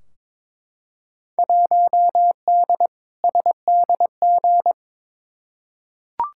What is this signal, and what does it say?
Morse code, heard as a keyed tone, about 22 wpm, 700 Hz. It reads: 1D SDG
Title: Field Day Exchanges - 22wpm